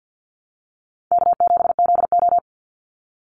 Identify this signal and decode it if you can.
Morse code, heard as a keyed tone, about 49 wpm, 700 Hz. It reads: K7ZO
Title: NA SOTA Chaser Callsigns 50wpm